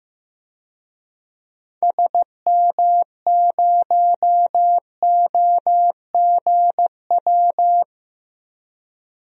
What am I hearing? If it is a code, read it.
Morse code, heard as a keyed tone, about 15 wpm, 700 Hz. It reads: SM0OGW
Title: DXCC-100 Call Signs - 6 Characters - 15wpm